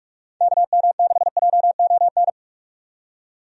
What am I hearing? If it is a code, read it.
Morse code, heard as a keyed tone, about 43 wpm, 700 Hz. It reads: KM6JON
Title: USA Call Signs - 6 Characters - 45wpm